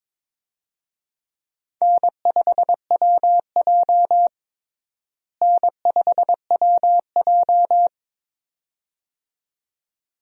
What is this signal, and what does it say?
Morse code, heard as a keyed tone, about 22 wpm, 700 Hz. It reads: N5WJ N5WJ
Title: USA Call Signs - 4 Characters - Repeated 2x - 22wpm